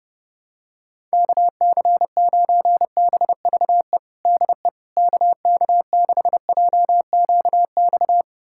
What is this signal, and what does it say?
Morse code, heard as a keyed tone, about 30 wpm, 700 Hz. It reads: KC9BVE DE KK6JQX
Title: USA Callsign DE Callsign - 6 Characters - 30wpm